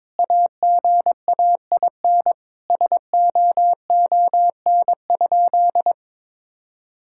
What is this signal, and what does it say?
Morse code, heard as a keyed tone, about 22 wpm, 700 Hz. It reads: AGAIN SOON?
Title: Sentences from Top 1000 Words - Review Part 6 - 22wpm